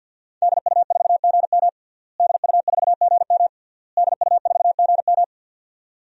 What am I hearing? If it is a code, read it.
Morse code, heard as a keyed tone, about 49 wpm, 700 Hz. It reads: DU4GM DU4GM DU4GM
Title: DXCC-100 Call Signs - 5 Characters - Repeated 3x - 50wpm